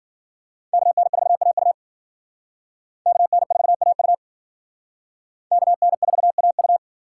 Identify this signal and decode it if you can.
Morse code, heard as a keyed tone, about 53 wpm, 700 Hz. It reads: KN4AU KN4AU KN4AU